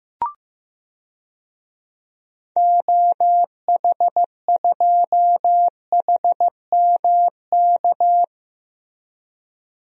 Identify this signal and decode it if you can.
Morse code, heard as a keyed tone, about 15 wpm, 700 Hz. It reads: OH2HMK
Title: DXCC-100 Call Signs - 6 Characters - Repeated 3x - 15wpm